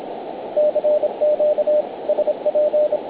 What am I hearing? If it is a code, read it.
Morse code, heard as a keyed tone, about 26 wpm, 600 Hz, with radio noise. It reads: CQ SP
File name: na7tb 160m 0900 gmt